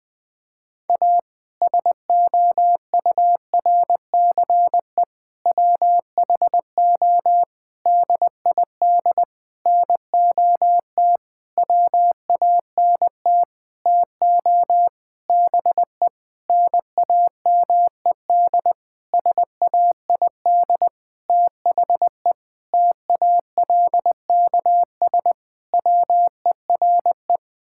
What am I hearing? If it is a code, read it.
Morse code, heard as a keyed tone, about 20 wpm, 700 Hz. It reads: A SOURCE WHO DID NOT WANT TO BE NAMED SAID THE TALKS WERE
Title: Sentences with 8-Word Phrases - 20wpm